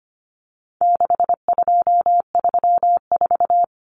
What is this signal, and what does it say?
Morse code, heard as a keyed tone, about 25 wpm, 700 Hz. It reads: 6234
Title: Common Cut Numbers - 4-Digit 25wpm